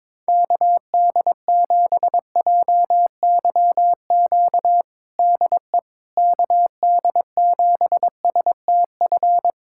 Morse code, heard as a keyed tone, about 22 wpm, 700 Hz. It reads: KD7JYQ DE KD7STF
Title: USA Callsign DE Callsign 22wpm